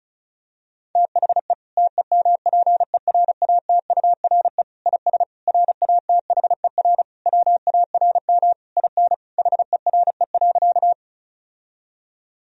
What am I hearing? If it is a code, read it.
Morse code, heard as a keyed tone, about 34 wpm, 700 Hz. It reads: THE TEMPERATURE IS RATHER WARM IN HERE.
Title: Sentences from Top 1000 Words - Review Part 4 - 35wpm